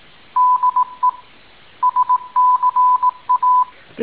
Morse code, heard as a keyed tone, about 18 wpm, 1000 Hz. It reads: DE SCA
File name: tin-upgrade / QSO 2019 08 20